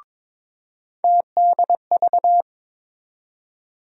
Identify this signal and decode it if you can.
Morse code, heard as a keyed tone, about 22 wpm, 700 Hz. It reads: TDV
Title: USA Call Sign Postfixes - 3 Letters 22wpm